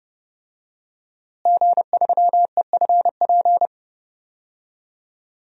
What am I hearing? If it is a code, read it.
Morse code, heard as a keyed tone, about 30 wpm, 700 Hz. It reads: G3EFP
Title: DXCC-100 Call Signs - 5 Characters - Repeated 3x - 30wpm